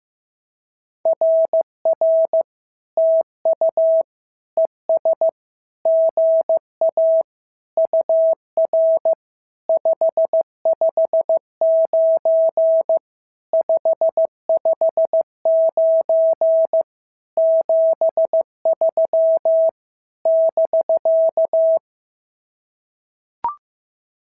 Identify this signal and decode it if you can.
Morse code, heard as a keyed tone, about 14 wpm, 650 Hz. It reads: RR TU ES GA UR 559 559 73 <BK>
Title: SOTA QSO Practice 15wpm